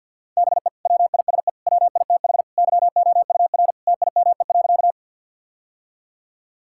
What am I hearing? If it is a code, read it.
Morse code, heard as a keyed tone, about 51 wpm, 700 Hz. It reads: BE WISE WITH YOUR TIME.
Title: Sentences from Top 1600 Words 50wpm